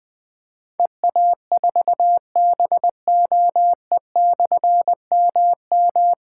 Morse code, heard as a keyed tone, about 20 wpm, 700 Hz. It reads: EA4BOE/MM
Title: Difficult Call Signs 20wpm